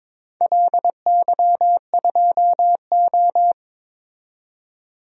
Morse code, heard as a keyed tone, about 22 wpm, 700 Hz. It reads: LY2O
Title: DXCC-100 Call Signs - 4 Characters - 22wpm